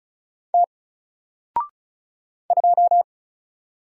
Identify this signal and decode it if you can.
Morse code, heard as a keyed tone, about 36 wpm, 700 Hz. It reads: T 2
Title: Single Letter-Number 35wpm